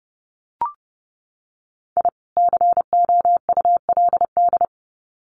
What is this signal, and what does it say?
Morse code, heard as a keyed tone, about 30 wpm, 700 Hz. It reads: I COULD
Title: Sets of 2 Words 30wpm